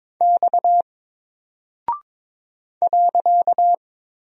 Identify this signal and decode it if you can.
Morse code, heard as a keyed tone, about 22 wpm, 700 Hz. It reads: X .
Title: Single Character (Letter:Number:Punctuation:Prosign) Alt 22wpm